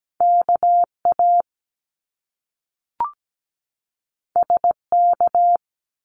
Morse code, heard as a keyed tone, about 17 wpm, 700 Hz. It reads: KA SK
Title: Top Two Letter Combinations in Words 17wpm